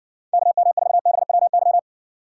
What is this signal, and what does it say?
Morse code, heard as a keyed tone, about 70 wpm, 700 Hz. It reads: KM4BWY